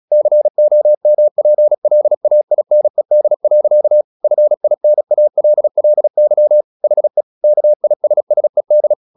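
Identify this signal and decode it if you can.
Morse code, heard as a keyed tone, about 36 wpm, 600 Hz. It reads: COMPLAINED. FINALLY HE KISSED
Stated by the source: Morse practice at 36 wpm, text of Ken Follett's novel Never, from CW med Gnister.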